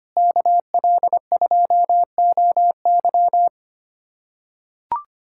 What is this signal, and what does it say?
Morse code, heard as a keyed tone, about 25 wpm, 700 Hz. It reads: KL2OY
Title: USA Callsign DE Callsign 25wpm